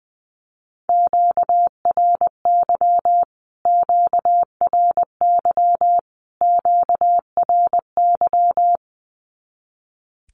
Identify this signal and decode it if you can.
Morse code, heard as a keyed tone, about 20 wpm, 700 Hz. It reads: QRY QRY QRY
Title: Q-Codes Repeated 3x 20wpm